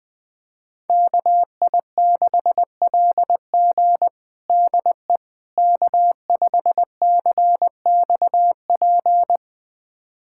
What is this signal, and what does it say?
Morse code, heard as a keyed tone, about 21 wpm, 700 Hz. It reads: KI6LG DE K5CXP
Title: USA Callsign DE Callsign - 5 Characters - 20wpm